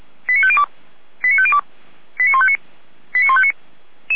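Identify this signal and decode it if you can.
Five-tone selective call (ZVEI), heard as quick runs of five tones, tones of about 70 ms. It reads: 89502 89502 89159 89159
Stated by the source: ZVEI five-tone call received by SDR, archived by Feuerwehr Orlamünde.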